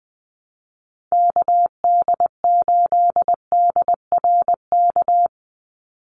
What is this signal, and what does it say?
Morse code, heard as a keyed tone, about 20 wpm, 700 Hz. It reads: KD8DRK
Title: USA Call Signs 20wpm